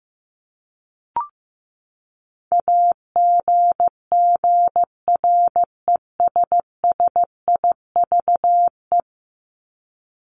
Morse code, heard as a keyed tone, about 15 wpm, 700 Hz. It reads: AGGRESSIVE
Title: Top 1501-1750 Words 15wpm